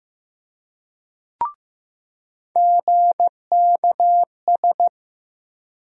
Morse code, heard as a keyed tone, about 15 wpm, 700 Hz. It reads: GKS
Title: USA Call Sign Postfixes - 3 Letters 15wpm